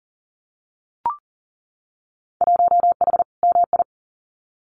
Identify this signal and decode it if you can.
Morse code, heard as a keyed tone, about 39 wpm, 700 Hz. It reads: 1H MI
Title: Winter Field Day Exchanges 40wpm